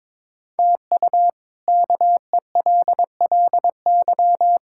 Morse code, heard as a keyed tone, about 22 wpm, 700 Hz. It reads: TU KELLY
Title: K1USN SST QSO Practice - 22wpm